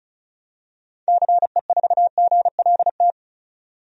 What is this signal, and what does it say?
Morse code, heard as a keyed tone, about 35 wpm, 700 Hz. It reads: CE4GLT
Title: DXCC-100 Call Signs - 6 Characters - 35wpm